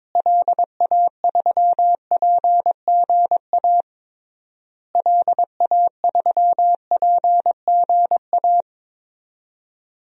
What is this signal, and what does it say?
Morse code, heard as a keyed tone, about 22 wpm, 700 Hz. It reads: LA3PGA LA3PGA
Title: DXCC-100 Call Signs - 6 Characters - Repeated 3x - 22wpm